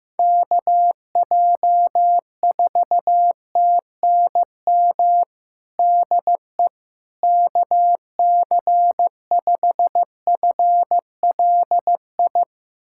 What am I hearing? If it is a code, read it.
Morse code, heard as a keyed tone, about 15 wpm, 700 Hz. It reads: KJ4TNM DE KC5FLI
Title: USA Callsign DE Callsign - 6 Characters - 15wpm